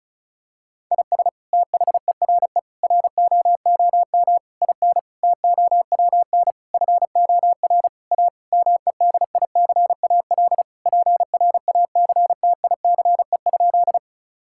Morse code, heard as a keyed tone, about 36 wpm, 700 Hz. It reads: IS THERE ROOM IN TOWN FOR A MEDICAL PRACTICE?
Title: Sentences from Top 1000 Words - Review Part 5 - 35wpm